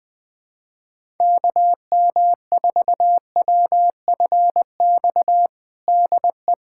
Morse code, heard as a keyed tone, about 21 wpm, 700 Hz. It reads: KM4WFX DE
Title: USA Callsign DE Callsign - 6 Characters - 20wpm